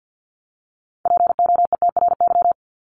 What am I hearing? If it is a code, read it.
Morse code, heard as a keyed tone, about 50 wpm, 700 Hz. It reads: POETRY
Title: Top 1751-2000 Words 50wpm